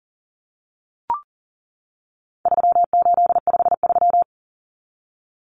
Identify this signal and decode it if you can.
Morse code, heard as a keyed tone, about 40 wpm, 700 Hz. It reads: 3853
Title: Common Cut Numbers - 4-Digit 40wpm